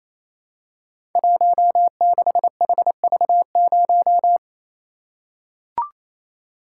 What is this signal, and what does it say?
Morse code, heard as a keyed tone, about 29 wpm, 700 Hz. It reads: 16HV0
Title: Code Groups - 5 Letter:Numbers - 28wpm